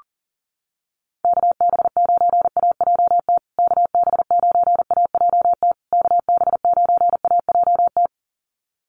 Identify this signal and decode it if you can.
Morse code, heard as a keyed tone, about 40 wpm, 700 Hz. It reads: KB9AJT KB9AJT KB9AJT
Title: USA Call Signs Repeated 3x 40wpm